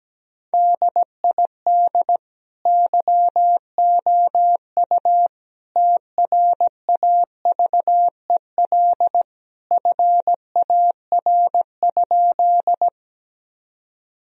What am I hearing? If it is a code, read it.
Morse code, heard as a keyed tone, about 17 wpm, 700 Hz. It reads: DID YOU TRAVEL FAR?
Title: Sentences from Top 1000 Words - Review Part 2 - 17wpm